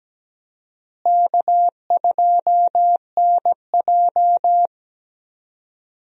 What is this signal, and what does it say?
Morse code, heard as a keyed tone, about 17 wpm, 700 Hz. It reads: K2NJ
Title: USA Call Signs - 4 Characters - Repeated 2x - 17wpm